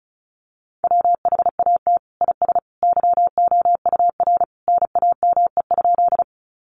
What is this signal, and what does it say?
Morse code, heard as a keyed tone, about 34 wpm, 700 Hz. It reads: WHAT IS YOUR NAME?
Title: Sentences from Top 200 Words 35wpm